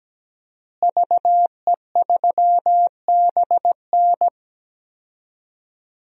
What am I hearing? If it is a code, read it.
Morse code, heard as a keyed tone, about 17 wpm, 700 Hz. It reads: VE3BN
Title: DXCC-100 Call Signs - 5 Characters - Repeated 2x - 17wpm